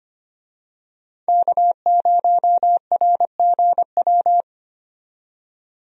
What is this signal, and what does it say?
Morse code, heard as a keyed tone, about 25 wpm, 700 Hz. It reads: K0RGW
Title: USA Call Signs - 5 Characters - 25wpm